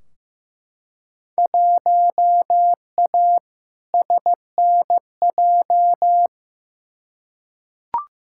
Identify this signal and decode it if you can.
Morse code, heard as a keyed tone, about 15 wpm, 700 Hz. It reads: 1A SNJ
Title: Field Day Exchanges - 15wpm